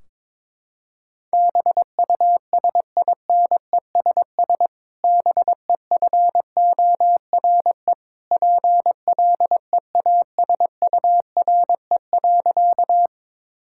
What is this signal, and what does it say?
Morse code, heard as a keyed tone, about 22 wpm, 700 Hz. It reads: BUSINESS BEFORE PLEASURE.
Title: Sentences from Top 1600 Words 22wpm